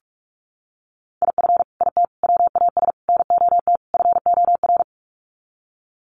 Morse code, heard as a keyed tone, about 44 wpm, 700 Hz. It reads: IF IT WAS NOT FOR
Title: Sets of 5 Words Encore 45wpm